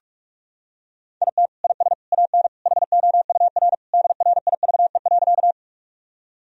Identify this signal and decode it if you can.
Morse code, heard as a keyed tone, about 45 wpm, 700 Hz. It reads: IT IS AN HOUR DRIVE.
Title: Sentences from Top 1000 Words - Review Part 2 - 45wpm